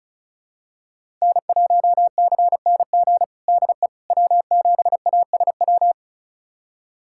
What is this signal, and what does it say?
Morse code, heard as a keyed tone, about 36 wpm, 700 Hz. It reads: N1CNG DE W7ASW
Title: USA Callsign DE Callsign - 5 Characters - 35wpm